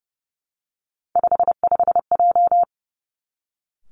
Morse code, heard as a keyed tone, about 30 wpm, 700 Hz. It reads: 55J
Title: Three Challenging Letter-Number Combinations 30wpm